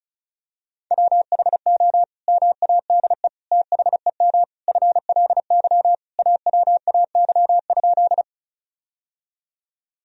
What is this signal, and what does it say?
Morse code, heard as a keyed tone, about 34 wpm, 700 Hz. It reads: WHO MADE THEM FLY AWAY?
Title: Sentences from Top 1000 Words - Review Part 2 - 35wpm